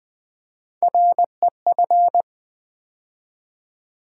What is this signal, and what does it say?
Morse code, heard as a keyed tone, about 20 wpm, 700 Hz. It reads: REF
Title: Build A Word - 3 Letters At A Time - 9-Letter Words - 20wpm